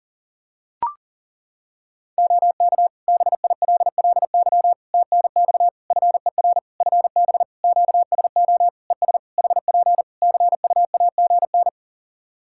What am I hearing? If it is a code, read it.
Morse code, heard as a keyed tone, about 38 wpm, 700 Hz. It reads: OK BILLY TNX FER FB QSO ES HP CUAGN
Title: Practice QSO - Exchange 4 - Ending - 40wpm